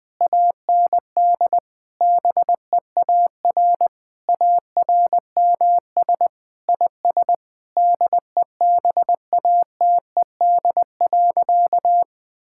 Morse code, heard as a keyed tone, about 20 wpm, 700 Hz. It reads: AND BEAR ARMS IS DEBATED.
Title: Sentences with 7-Word Phrases - Spoken Beginning - 20wpm